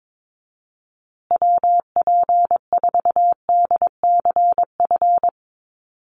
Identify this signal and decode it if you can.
Morse code, heard as a keyed tone, about 22 wpm, 700 Hz. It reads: WP4DCF
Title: USA Call Signs Repeated 2x 22wpm